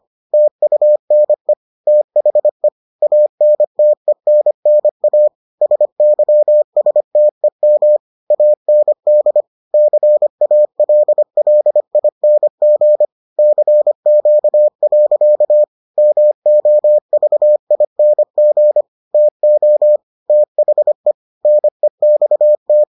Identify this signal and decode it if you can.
Morse code, heard as a keyed tone, about 25 wpm, 600 Hz. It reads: TUNE THE ANTENNA SYSTEM AND CALLING CQ. MOVING TO THE NEXT